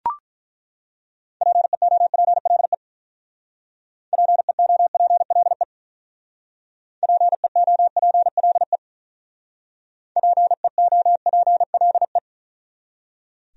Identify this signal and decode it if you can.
Morse code, heard as a keyed tone, about 43 wpm, 700 Hz. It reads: PEOPLE PEOPLE PEOPLE PEOPLE
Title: Speed-Racing Top 200 Words 35wpm